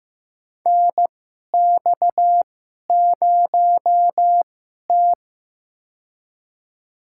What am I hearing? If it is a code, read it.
Morse code, heard as a keyed tone, about 15 wpm, spaced as for 11 wpm, 700 Hz. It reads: NX0T
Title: USA Call Signs - 4 Characters - 2x Character Spacing - 15wpm